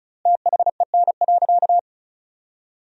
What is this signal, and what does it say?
Morse code, heard as a keyed tone, about 36 wpm, 700 Hz. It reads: THEN.
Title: 4x Word-Spacing - Sentences from Top 200 Words - 35wpm